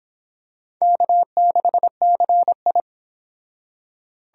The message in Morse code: K6CI

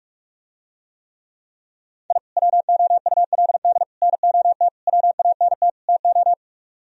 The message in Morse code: I WOULD NOT WANT TO

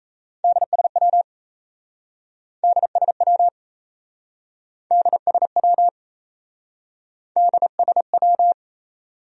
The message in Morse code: DSW DSW DSW DSW

